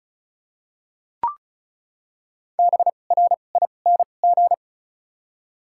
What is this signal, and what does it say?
Morse code, heard as a keyed tone, about 34 wpm, spaced as for 25 wpm, 700 Hz. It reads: BRING
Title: 2x Character Spacing - Top 500 Words - 35wpm